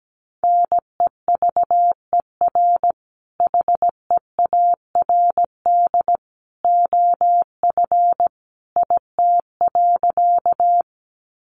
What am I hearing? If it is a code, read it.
Morse code, heard as a keyed tone, about 17 wpm, 700 Hz. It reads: NEVER HEARD OF IT.
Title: Sentences from Top 400 Words 17wpm